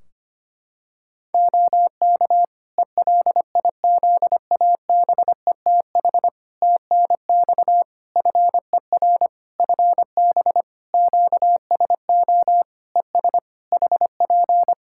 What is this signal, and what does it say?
Morse code, heard as a keyed tone, about 25 wpm, 700 Hz. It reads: OK ELIZABETH TNX FER FB QSO ES HP
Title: Practice QSO - Exchange 4 - Ending - 25wpm